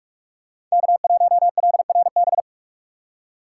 Morse code, heard as a keyed tone, about 46 wpm, 700 Hz. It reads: K1LRB
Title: USA Call Signs - 5 Characters - 4x Character Spacing - 45wpm